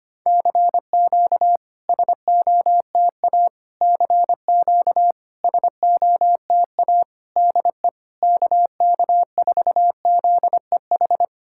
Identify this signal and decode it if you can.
Morse code, heard as a keyed tone, about 25 wpm, 700 Hz. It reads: CQ SOTA CQ SOTA DE KK4ZEH
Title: SOTA QSO Practice 25wpm